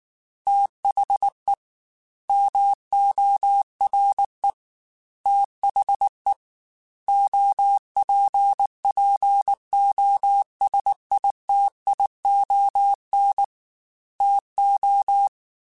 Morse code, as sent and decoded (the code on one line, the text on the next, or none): - .... / -- --- .-. / - .... / --- .--. .--. --- ... .. - .. --- -. / - ---
THE MORE THE OPPOSITION TO